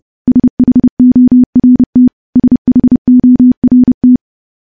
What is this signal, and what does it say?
Morse code, heard as a keyed tone, about 30 wpm, 250 Hz. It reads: SHORT SHORT
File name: morse-training-30WPM